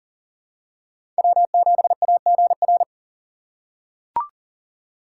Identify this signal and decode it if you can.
Morse code, heard as a keyed tone, about 40 wpm, 700 Hz. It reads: W7AGR
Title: USA Call Signs - 5 Characters - 2x Character Spacing - 40wpm